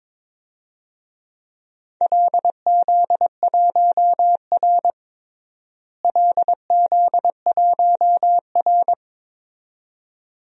Morse code, heard as a keyed tone, about 22 wpm, 700 Hz. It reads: LZ1R LZ1R
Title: DXCC-100 Call Signs - 4 Characters - Repeated 2x - 22wpm